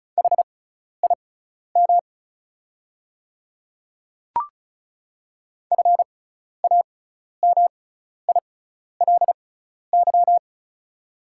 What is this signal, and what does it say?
Morse code, heard as a keyed tone, about 36 wpm, spaced as for 12 wpm, 700 Hz. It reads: HIM FAMILY
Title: 6x Character Spacing - Top 500 Words - 35wpm